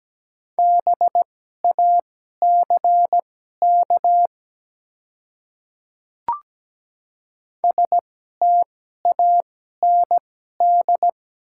Morse code, heard as a keyed tone, about 17 wpm, spaced as for 12 wpm, 700 Hz. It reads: BACK STAND
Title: 2x Character Spacing - Top 300 Words - 17wpm